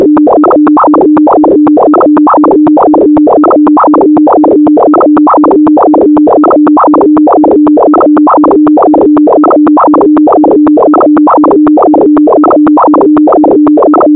Two-tone teleprinter signal, about 200 Hz shift.
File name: thet / radia297 / radia s24 n297 radiohelsinki-raggam--dataradio-pt4